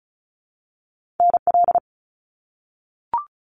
Morse code, heard as a keyed tone, about 36 wpm, 700 Hz. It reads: NL
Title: Canadian Provinces and Territories - Repeated 3x - 35wpm